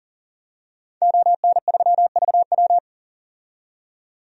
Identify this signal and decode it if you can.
Morse code, heard as a keyed tone, about 40 wpm, 700 Hz. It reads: ON3VW
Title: DXCC-100 Call Signs - 5 Characters - 40wpm